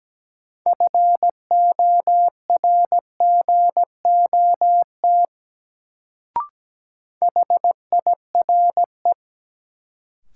Morse code, heard as a keyed tone, about 17 wpm, 700 Hz. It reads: FORGOT HIRE
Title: Sentences from Top 1400 Words - New Word Drill Alt - 17wpm